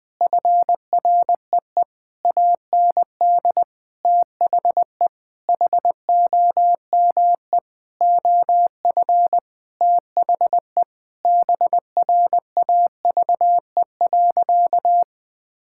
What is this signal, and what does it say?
Morse code, heard as a keyed tone, about 20 wpm, 700 Hz. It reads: FREE AND THE HOME OF THE BRAVE.
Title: Sentences with 9-Word Phrases - 20wpm